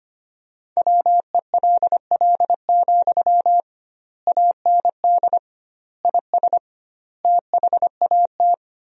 Morse code, heard as a keyed tone, about 25 wpm, 700 Hz. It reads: WELL, AND IS THAT
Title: 2x Word-Spacing - Sentences from Top 200 Words - 25wpm